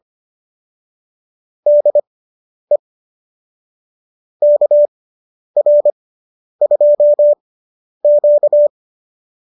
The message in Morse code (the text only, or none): DE KR2Q